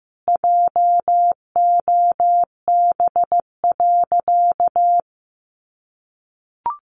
Morse code, heard as a keyed tone, about 15 wpm, 700 Hz. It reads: JOB.
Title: Sentences from Top 700 Words 15wpm